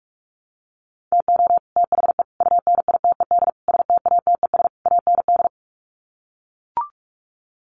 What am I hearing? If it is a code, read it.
Morse code, heard as a keyed tone, about 44 wpm, 700 Hz. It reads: TO THE UNITED STATES AND
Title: Sets of 5 Words Encore 45wpm